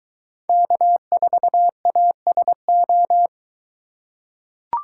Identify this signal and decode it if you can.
Morse code, heard as a keyed tone, about 23 wpm, 700 Hz. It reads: K4ASO